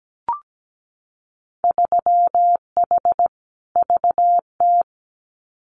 Morse code, heard as a keyed tone, about 17 wpm, 700 Hz. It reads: 3H VT
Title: Winter Field Day Exchanges Encore 17wpm